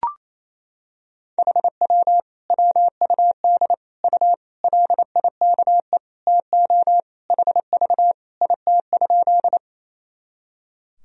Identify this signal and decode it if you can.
Morse code, heard as a keyed tone, about 28 wpm, 700 Hz. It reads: HW WUD U LIKE TO HV IT?
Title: Sentences from Top 100 Words and Abbreviations 28wpm